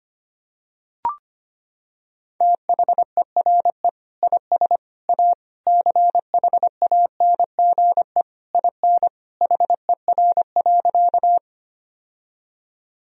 Morse code, heard as a keyed tone, about 25 wpm, 700 Hz. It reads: THERE IS A CHANGE IN HER.
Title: Sentences from Top 1000 Words - Review Part 1 - 25wpm